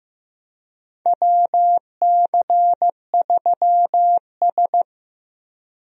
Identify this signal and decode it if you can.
Morse code, heard as a keyed tone, about 15 wpm, 700 Hz. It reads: WC3S
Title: USA Call Signs - 4 Characters - Repeated 3x - 15wpm